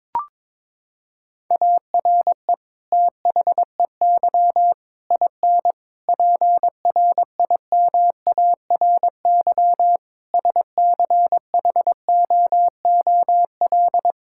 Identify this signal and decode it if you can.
Morse code, heard as a keyed tone, about 22 wpm, 700 Hz. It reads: ARE THEY IN PRIMARY SCHOOL
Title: Sentences from Top 1000 Words - Review Part 5 - 22wpm